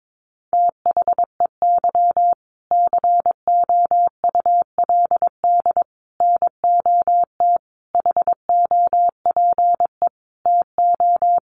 Morse code, heard as a keyed tone, about 22 wpm, 700 Hz. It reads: THEY COULD NOT HOPE TO
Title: Sentences from Top 1100 Words 22wpm